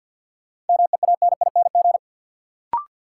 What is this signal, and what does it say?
Morse code, heard as a keyed tone, about 53 wpm, 700 Hz. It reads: MEANING
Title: Sentences from Top 900 Words - New Word Drill - 50wpm